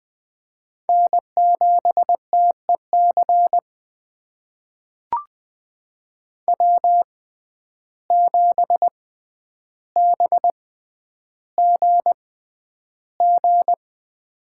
Morse code, heard as a keyed tone, about 21 wpm, spaced as for 10 wpm, 700 Hz. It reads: N7TEC W7BGG
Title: USA Call Signs - 6x Character Spacing - 20wpm